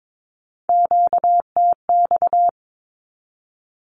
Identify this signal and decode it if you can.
Morse code, heard as a keyed tone, about 22 wpm, 700 Hz. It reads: QTX
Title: Q-Codes 22wpm